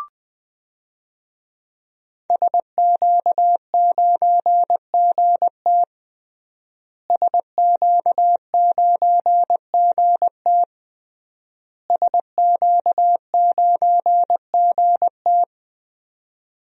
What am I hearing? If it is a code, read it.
Morse code, heard as a keyed tone, about 20 wpm, 700 Hz. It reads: SQ9GT SQ9GT SQ9GT
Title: DXCC-100 Call Signs - 5 Characters - Repeated 3x - 20wpm